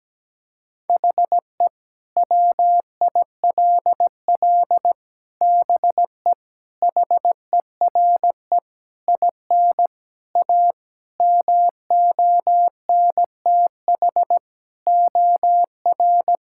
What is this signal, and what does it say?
Morse code, heard as a keyed tone, about 17 wpm, 700 Hz. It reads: HE WILL BE HERE IN A MONTH OR
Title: Sentences from Top 1000 Words - Review Part 5 - 17wpm